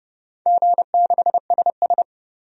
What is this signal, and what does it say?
Morse code, heard as a keyed tone, about 30 wpm, 700 Hz. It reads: G6SS
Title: DXCC-100 Call Signs - 4 Characters - 30wpm